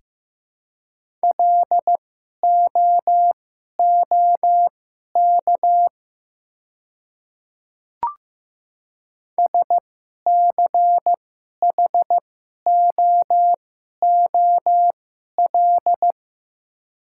Morse code, heard as a keyed tone, about 16 wpm, spaced as for 11 wpm, 700 Hz. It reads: LOOK SCHOOL
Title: 2x Character Spacing - Top 300 Words - 15wpm